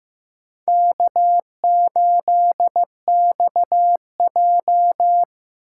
Morse code, heard as a keyed tone, about 16 wpm, 700 Hz. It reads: K8XJ
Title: K1USN SST QSO Practice - 15wpm